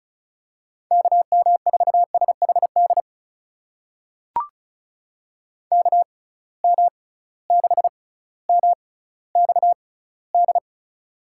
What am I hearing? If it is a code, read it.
Morse code, heard as a keyed tone, about 36 wpm, spaced as for 18 wpm, 700 Hz. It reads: KM4SHD KM6MXD
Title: USA Call Signs - 6 Characters - 6x Character Spacing - 35wpm